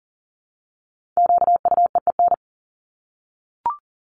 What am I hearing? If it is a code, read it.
Morse code, heard as a keyed tone, about 40 wpm, 700 Hz. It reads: QUEEN
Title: Top 1751-2000 Words 40wpm